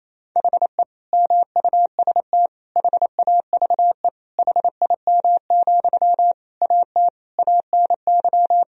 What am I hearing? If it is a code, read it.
Morse code, heard as a keyed tone, about 28 wpm, 700 Hz. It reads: HE MUST HAVE HIM, AT ANY
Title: Sentences from Top 1000 Words - Review Part 4 - 28wpm